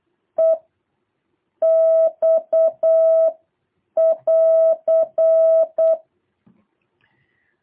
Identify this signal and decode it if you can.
Morse code, heard as a keyed tone, about 8 wpm, 660 Hz. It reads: E X <AR>
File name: jh8cbh / 2023cw